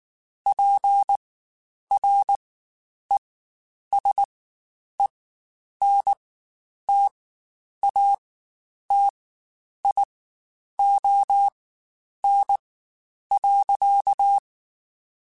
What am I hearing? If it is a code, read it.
Morse code, heard as a keyed tone, about 19 wpm, spaced as for 9 wpm, 800 Hz. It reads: PRESENTATION.
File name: Quotes Spaced / Quotes Spaced-mp3s / Persperation sp @18